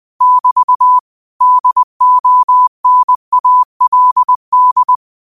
Morse code, heard as a keyed tone, about 20 wpm, 1000 Hz. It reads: <BT> DONALD